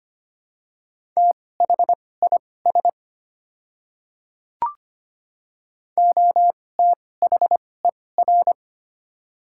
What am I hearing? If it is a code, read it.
Morse code, heard as a keyed tone, about 25 wpm, spaced as for 18 wpm, 700 Hz. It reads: THIS OTHER
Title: 2x Character Spacing - Top 100 Words - 25wpm